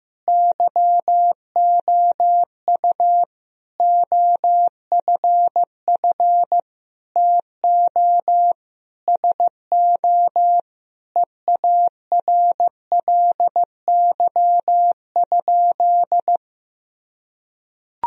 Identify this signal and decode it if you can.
Morse code, heard as a keyed tone, about 15 wpm, 700 Hz. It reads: YOU OFF TO SO EARLY?
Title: Sentences from Top 1000 Words - Review Part 2 - 15wpm